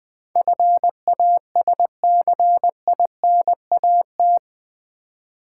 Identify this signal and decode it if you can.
Morse code, heard as a keyed tone, about 20 wpm, 700 Hz. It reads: FASCINAT
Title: Build A Word - 4 Letters At A Time - 11-Letter Words - 20wpm